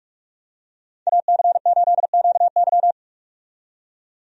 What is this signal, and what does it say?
Morse code, heard as a keyed tone, about 46 wpm, 700 Hz. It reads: AC8QY
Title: USA Call Signs - 5 Characters - 45wpm